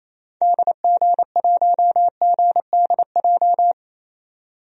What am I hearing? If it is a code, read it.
Morse code, heard as a keyed tone, about 28 wpm, 700 Hz. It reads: DG1GDJ
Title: DXCC-100 Call Signs - 6 Characters - 28wpm